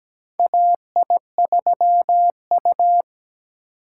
Morse code, heard as a keyed tone, about 17 wpm, 700 Hz. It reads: AI3U
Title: USA Call Signs - 4 Characters - 8x Character Spacing - 17wpm